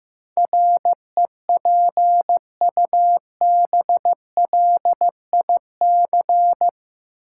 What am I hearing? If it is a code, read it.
Morse code, heard as a keyed tone, about 15 wpm, 700 Hz. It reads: REPUBLIC
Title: Sentences from Top 1500 Words - New Word Drill - 15wpm